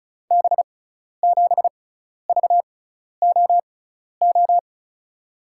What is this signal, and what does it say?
Morse code, heard as a keyed tone, about 36 wpm, spaced as for 12 wpm, 700 Hz. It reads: B7VOO
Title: USA Call Signs - 6 Characters - 6x Character Spacing - 35wpm